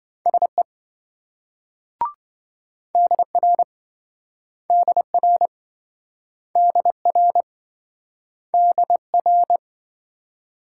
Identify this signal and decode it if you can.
Morse code, heard as a keyed tone, about 26 wpm, 700 Hz. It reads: SE DR DR DR DR